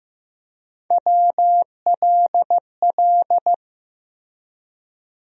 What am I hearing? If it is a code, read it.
Morse code, heard as a keyed tone, about 16 wpm, 700 Hz. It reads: WLL
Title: USA Call Sign Postfixes - 3 Letters 15wpm